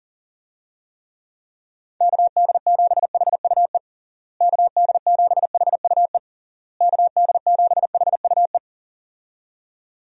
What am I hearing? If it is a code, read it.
Morse code, heard as a keyed tone, about 37 wpm, 700 Hz. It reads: KD7HUE KD7HUE KD7HUE
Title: USA Call Signs - 6 Characters - Repeated 3x - 40wpm